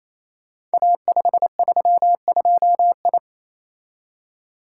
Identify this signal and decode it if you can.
Morse code, heard as a keyed tone, about 28 wpm, 700 Hz. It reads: A532I
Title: Code Groups - 5 Letter:Numbers - 28wpm